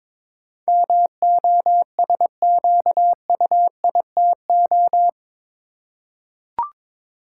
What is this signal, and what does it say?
Morse code, heard as a keyed tone, about 22 wpm, 700 Hz. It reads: MOSQUITO
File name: practice / 22wpm / Eight Letter Words - 22wpm